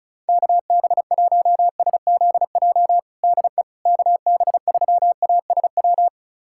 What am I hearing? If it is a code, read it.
Morse code, heard as a keyed tone, about 36 wpm, 700 Hz. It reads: KB1SZJ DE KB3ASW
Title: USA Callsign DE Callsign - 6 Characters - 35wpm